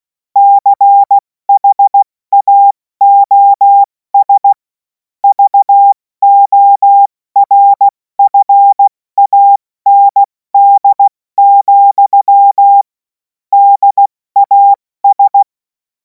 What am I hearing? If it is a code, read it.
Morse code, heard as a keyed tone, about 16 wpm, 800 Hz. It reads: CHAOS VORFAND, DAS
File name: Morsekurs / Witze / Witz 24. Februar 2026 70 F80